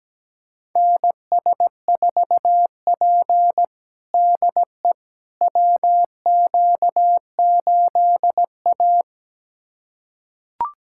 Morse code, heard as a keyed tone, about 17 wpm, 700 Hz. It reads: NS4P DE WQ8A
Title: USA Callsign DE Callsign - 4 Characters - 17wpm